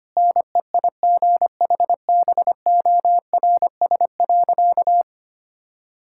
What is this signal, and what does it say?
Morse code, heard as a keyed tone, about 25 wpm, 700 Hz. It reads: NEIGHBORS.
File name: practice / 25wpm / Sentences with 4-Word Phrases - 25wpm